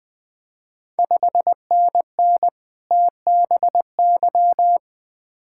Morse code, heard as a keyed tone, about 20 wpm, 700 Hz. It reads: 5NN TBY
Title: Ontario QSO Party Exchange 20wpm